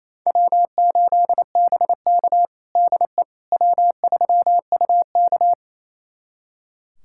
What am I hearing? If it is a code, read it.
Morse code, heard as a keyed tone, about 28 wpm, 700 Hz. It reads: W8BK DE W3UK
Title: USA Callsign DE Callsign - 4 Characters - 28wpm